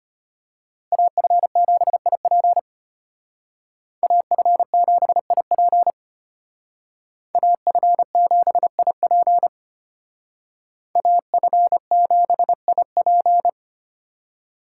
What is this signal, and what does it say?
Morse code, heard as a keyed tone, about 32 wpm, 700 Hz. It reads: AF7IP AF7IP AF7IP AF7IP